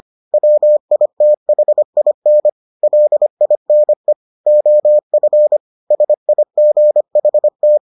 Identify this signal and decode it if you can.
Morse code, heard as a keyed tone, about 25 wpm, 600 Hz. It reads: WITHIN LINE OF SIGHT